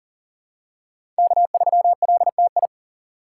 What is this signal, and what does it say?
Morse code, heard as a keyed tone, about 40 wpm, 700 Hz. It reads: K3LTI
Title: USA Call Signs - 5 Characters - 2x Character Spacing - 40wpm